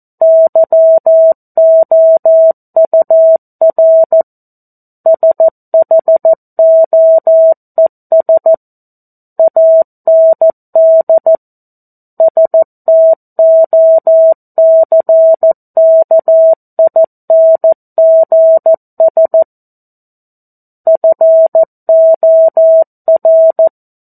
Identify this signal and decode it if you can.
Morse code, heard as a keyed tone, about 14 wpm, 650 Hz. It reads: YOUR SHOES AND STOCKINGS FOR
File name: 02-Chapter-02 14 WPM